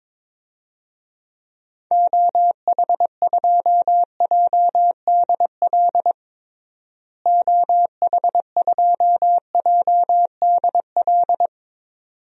Morse code, heard as a keyed tone, about 22 wpm, 700 Hz. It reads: OH2JDL OH2JDL
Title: DXCC-100 Call Signs - 6 Characters - Repeated 2x - 22wpm